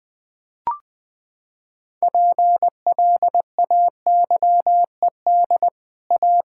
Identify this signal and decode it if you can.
Morse code, heard as a keyed tone, about 20 wpm, 700 Hz. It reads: PLAYED A
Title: Sets of 6 Words - 20wpm